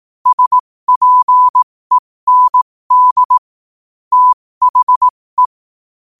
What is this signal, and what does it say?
Morse code, heard as a keyed tone, about 18 wpm, spaced as for 15 wpm, 1000 Hz. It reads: SPEND THE
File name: qotdfiles / 15wpm